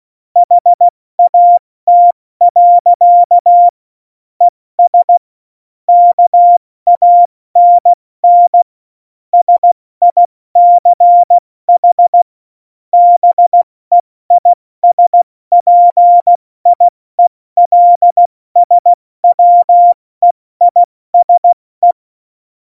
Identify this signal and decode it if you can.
Morse code, heard as a keyed tone, about 16 wpm, 700 Hz. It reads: HAT. ES KANN SICH BEISPIELSWEISE
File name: Morsekurs / Witze / Witz 14 Juli 2020, 70 F80